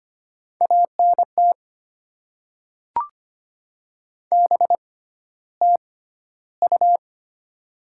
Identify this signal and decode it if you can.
Morse code, heard as a keyed tone, about 25 wpm, spaced as for 13 wpm, 700 Hz. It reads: ANT BTU
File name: Top 20 QSO Elements - 6x Character Spacing - 25wpm